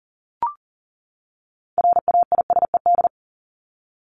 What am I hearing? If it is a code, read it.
Morse code, heard as a keyed tone, about 40 wpm, 700 Hz. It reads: RAISED
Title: ED Suffix - Top 100 Words with Base Words 40wpm